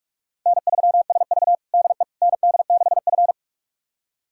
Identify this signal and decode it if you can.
Morse code, heard as a keyed tone, about 43 wpm, 700 Hz. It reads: N3SV DE ND6F
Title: USA Callsign DE Callsign - 4 Characters - 45wpm